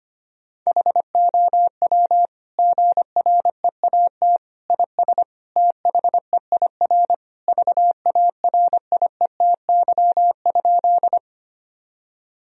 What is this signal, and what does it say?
Morse code, heard as a keyed tone, about 25 wpm, 700 Hz. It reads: HOW GREAT IS THEIR VARIETY?
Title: Sentences from Top 1000 Words - Review Part 4 - 25wpm